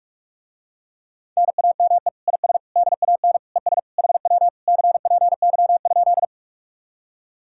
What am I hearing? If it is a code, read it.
Morse code, heard as a keyed tone, about 45 wpm, 700 Hz. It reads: NAME IS DAN ES HW CPY?